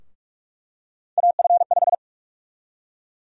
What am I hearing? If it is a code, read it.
Morse code, heard as a keyed tone, about 42 wpm, 700 Hz. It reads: AF5
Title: USA Call Sign Prefixes - Repeated 2x - 45wpm